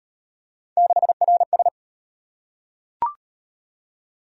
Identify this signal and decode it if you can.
Morse code, heard as a keyed tone, about 36 wpm, 700 Hz. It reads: 6RS